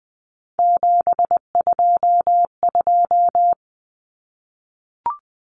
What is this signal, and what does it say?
Morse code, heard as a keyed tone, about 20 wpm, 700 Hz. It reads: 722
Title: Numbers - 3-Digit 20wpm